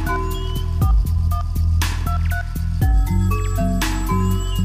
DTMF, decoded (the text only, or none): 4123